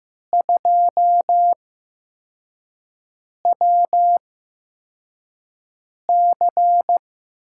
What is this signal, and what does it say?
Morse code, heard as a keyed tone, about 15 wpm, spaced as for 4 wpm, 700 Hz. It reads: 2WC
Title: USA Call Signs - 4 Characters - 8x Character Spacing - 15wpm